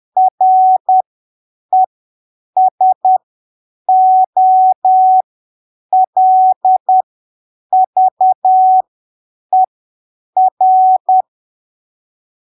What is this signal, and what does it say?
Morse code, heard as a keyed tone, about 10 wpm, spaced as for 7 wpm, 750 Hz. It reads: RESOLVER